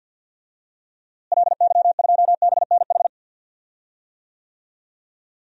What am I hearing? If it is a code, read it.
Morse code, heard as a keyed tone, about 47 wpm, 700 Hz. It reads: LY2BNH
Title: DXCC-100 Call Signs - 6 Characters - Repeated 3x - 50wpm